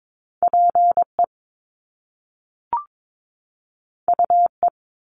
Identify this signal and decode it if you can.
Morse code, heard as a keyed tone, about 22 wpm, 700 Hz. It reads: PE UE
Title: Top Two Letter Combinations in Words 22wpm